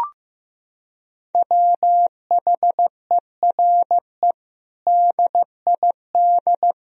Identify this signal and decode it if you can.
Morse code, heard as a keyed tone, about 15 wpm, 700 Hz. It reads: WHERE DID
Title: Sentences from Top 200 Words and Abbreviations 15wpm